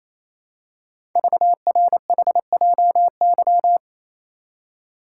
Morse code, heard as a keyed tone, about 27 wpm, 700 Hz. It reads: VRHJY
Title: Code Groups - 5 Letter:Numbers - 28wpm